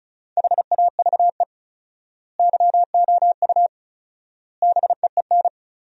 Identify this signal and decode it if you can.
Morse code, heard as a keyed tone, about 34 wpm, 700 Hz. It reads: HAVE YOU BEEN
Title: 4x Word-Spacing - Sentences from Top 300 Words - 35wpm